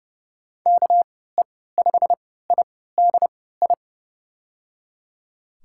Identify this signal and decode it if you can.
Morse code, heard as a keyed tone, about 30 wpm, spaced as for 17 wpm, 700 Hz. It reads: KE5IDI
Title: USA Call Signs - 6 Characters - 3x Character Spacing - 30wpm